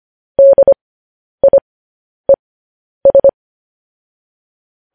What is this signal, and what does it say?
Morse code, heard as a keyed tone, about 25 wpm, spaced as for 10 wpm, 550 Hz. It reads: DIES
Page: news headlines in morse code at 10 wpm